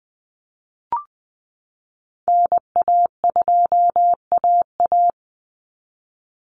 Morse code, heard as a keyed tone, about 20 wpm, 700 Hz. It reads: NA2AA
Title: Call Signs Repeated 2x - 20wpm